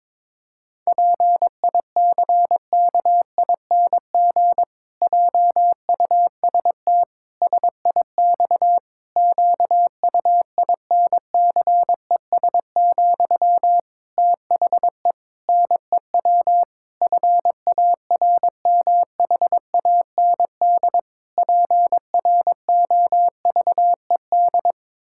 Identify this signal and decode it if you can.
Morse code, heard as a keyed tone, about 22 wpm, 700 Hz. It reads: PICKING JUST SIX QUINCES, THE NEW FARMHAND PROVED
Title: Pangram Challenge v2 - 22wpm